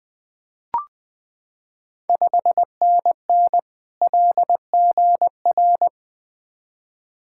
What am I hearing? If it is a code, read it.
Morse code, heard as a keyed tone, about 20 wpm, 700 Hz. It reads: 5NN LGR
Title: Ontario QSO Party Exchange 20wpm